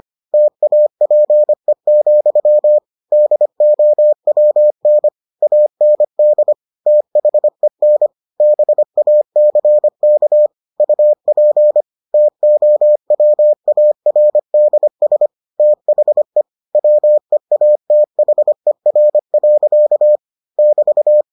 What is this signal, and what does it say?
Morse code, heard as a keyed tone, about 25 wpm, 600 Hz. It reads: TAPE, DOWN AND THEN BACK UP TOWARDS THE WEATHER. <BT>